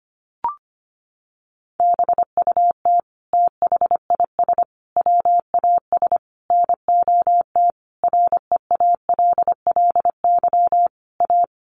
Morse code, heard as a keyed tone, about 25 wpm, 700 Hz. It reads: BUT THIS WAS NOT REALLY A
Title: Sentences from Top 1500 Words 25wpm